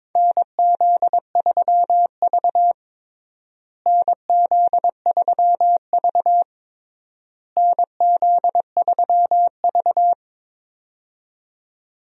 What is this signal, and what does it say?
Morse code, heard as a keyed tone, about 22 wpm, 700 Hz. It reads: NZ3V NZ3V NZ3V
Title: USA Call Signs - 4 Characters - Repeated 3x - 22wpm